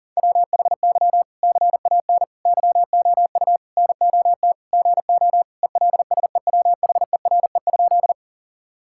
Morse code, heard as a keyed tone, about 37 wpm, 700 Hz. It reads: WHY CAN YOU NOT GO ELSEWHERE?
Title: Sentences from Top 1600 Words 40wpm